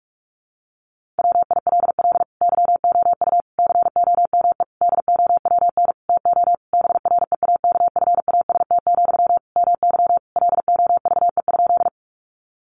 Morse code, heard as a keyed tone, about 44 wpm, 700 Hz. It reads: WILL YOU COME DOWN TO BREAKFAST, MY LOVE?
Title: Sentences from Top 1500 Words 45wpm